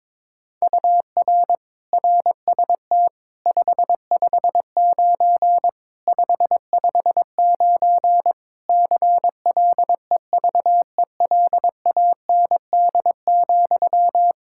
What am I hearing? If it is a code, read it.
Morse code, heard as a keyed tone, about 22 wpm, 700 Hz. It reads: UR RST 559 559 CLEVELAND,
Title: K1USN SST QSO Practice 22wpm